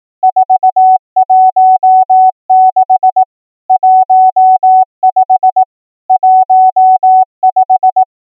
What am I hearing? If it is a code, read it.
Morse code, heard as a keyed tone, about 18 wpm, 750 Hz. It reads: 416 15 15
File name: numbers / V2-morse